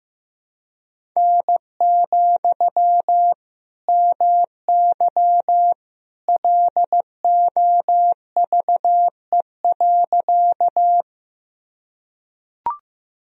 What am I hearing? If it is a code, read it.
Morse code, heard as a keyed tone, about 16 wpm, 700 Hz. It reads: N, MY LOVE.
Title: Sentences from Top 400 Words and Abbreviations 15wpm